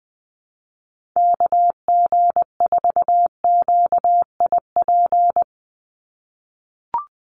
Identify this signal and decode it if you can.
Morse code, heard as a keyed tone, about 20 wpm, 700 Hz. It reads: KG4QIP
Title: USA Call Signs 20wpm